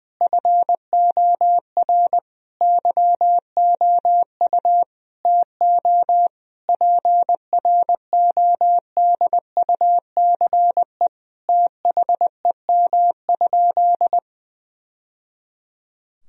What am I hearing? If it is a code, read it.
Morse code, heard as a keyed tone, about 20 wpm, 700 Hz. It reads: FOR YOU TO PRODUCE THEM?
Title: Sentences from Top 1000 Words - Review Part 3 - 20wpm